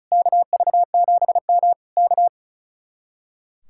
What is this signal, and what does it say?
Morse code, heard as a keyed tone, about 33 wpm, 700 Hz. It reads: KV7M K
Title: Speed-Racing Common QSO Phrases - 35wpm